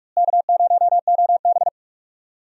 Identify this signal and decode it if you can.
Morse code, heard as a keyed tone, about 46 wpm, 700 Hz. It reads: K0OB
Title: USA Call Signs - 4 Characters - Repeated 3x - 45wpm